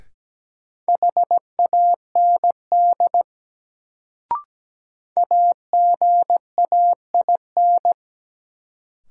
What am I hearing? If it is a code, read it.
Morse code, heard as a keyed tone, about 17 wpm, 700 Hz. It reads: HAND AGAIN
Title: Sentences from Top 200 Words - New Word Drill - 17wpm